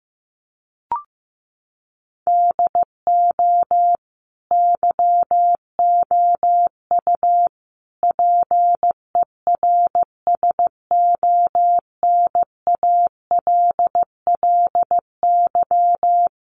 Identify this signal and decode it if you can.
Morse code, heard as a keyed tone, about 15 wpm, 700 Hz. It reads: DO YOU PERSONALLY
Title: Sentences from Top 1400 Words 15wpm